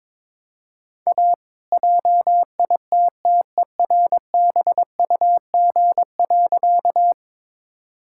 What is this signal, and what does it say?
Morse code, heard as a keyed tone, about 22 wpm, 690 Hz. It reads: A JITTERBUG.
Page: morse-code-ninja-practice-22wpm